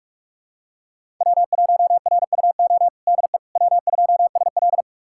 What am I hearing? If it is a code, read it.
Morse code, heard as a keyed tone, about 42 wpm, 700 Hz. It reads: W1RUO DE W2SL
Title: USA Callsign DE Callsign 45wpm